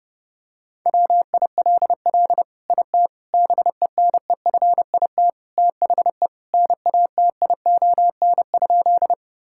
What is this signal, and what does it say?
Morse code, heard as a keyed tone, about 30 wpm, 700 Hz. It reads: WILL IT BENEFIT THE NATION?
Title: Sentences from Top 1000 Words - Review Part 6 - 30wpm